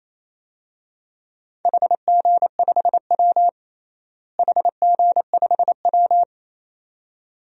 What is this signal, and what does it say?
Morse code, heard as a keyed tone, about 27 wpm, 700 Hz. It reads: HG5W HG5W
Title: DXCC-100 Call Signs - 4 Characters - Repeated 2x - 28wpm